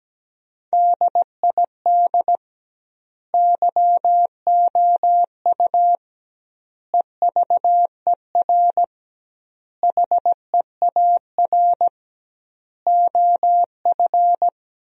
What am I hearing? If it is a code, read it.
Morse code, heard as a keyed tone, about 17 wpm, 700 Hz. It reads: DID YOU EVER HEAR OF
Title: 2x Word-Spacing - Sentences from Top 400 Words - 17wpm